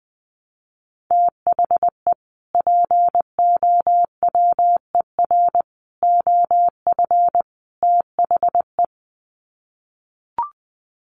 Morse code, heard as a keyed tone, about 20 wpm, 700 Hz. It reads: THE POWER OF THE
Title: Sets of 4 Words 20wpm